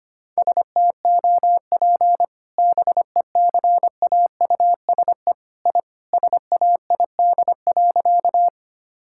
Morse code, heard as a keyed tone, about 25 wpm, 700 Hz. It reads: STOP BECAUSE I SAID.
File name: practice / 25wpm / Sentences with 2-Word Phrases - 25wpm